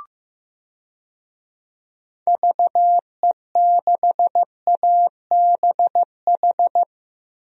Morse code, heard as a keyed tone, about 15 wpm, 700 Hz. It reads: VE6ABH
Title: DXCC-100 Call Signs - 6 Characters - 15wpm